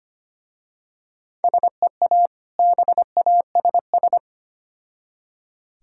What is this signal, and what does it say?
Morse code, heard as a keyed tone, about 25 wpm, 700 Hz. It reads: SEA BASS
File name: practice / 25wpm / Difficult Words with Hints - 25wpm